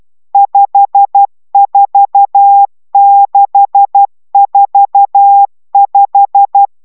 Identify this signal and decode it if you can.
Morse code, heard as a keyed tone, about 12 wpm, 800 Hz. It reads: 54645
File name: taviro tanulas / cw 69 20 10